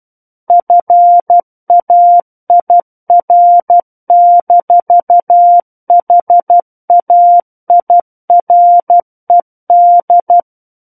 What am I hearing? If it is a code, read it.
Morse code, heard as a keyed tone, about 12 wpm, 700 Hz. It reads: FAIR-HAIRED
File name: Book One / 03-Chapter 3 - 12 WPM